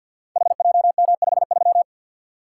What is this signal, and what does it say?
Morse code, heard as a keyed tone, about 46 wpm, 700 Hz. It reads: HJM53
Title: Code Groups - 5 Letter:Numbers - 50wpm